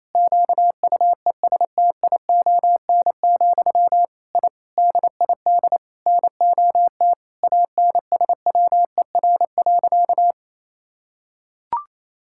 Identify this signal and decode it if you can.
Morse code, heard as a keyed tone, about 28 wpm, 700 Hz. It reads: QUESTION, I DID NOT ANSWER.
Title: Sentences from Top 1000 Words - Review Part 6 - 28wpm